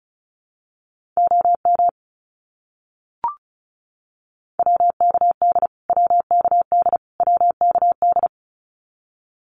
Morse code, heard as a keyed tone, about 34 wpm, 700 Hz. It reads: OM WKD WKD WKD
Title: Abbreviations Repeated 3x - 35wpm